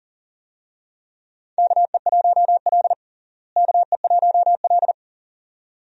Morse code, heard as a keyed tone, about 40 wpm, 700 Hz. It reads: KE1L KE1L
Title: USA Call Signs - 4 Characters - Repeated 2x - 40wpm